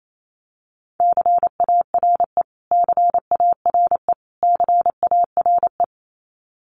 Morse code, heard as a keyed tone, about 28 wpm, 700 Hz. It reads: CARE CARE CARE
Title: Top 500 Words Repeated 28wpm